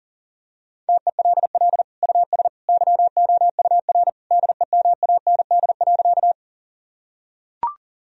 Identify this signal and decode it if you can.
Morse code, heard as a keyed tone, about 40 wpm, 700 Hz. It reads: TELL US YOUR DEMAND.
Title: Sentences from Top 1000 Words - Review Part 4 - 40wpm